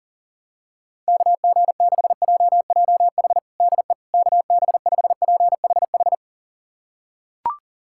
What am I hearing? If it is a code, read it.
Morse code, heard as a keyed tone, about 37 wpm, 700 Hz. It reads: KG6JJH DE KB5PHH
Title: USA Callsign DE Callsign 40wpm